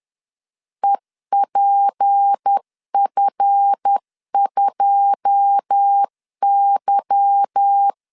Morse code, heard as a keyed tone, about 10 wpm, 800 Hz. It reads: EPF2Y
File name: LEZIONI / 9 286 10wpm 2735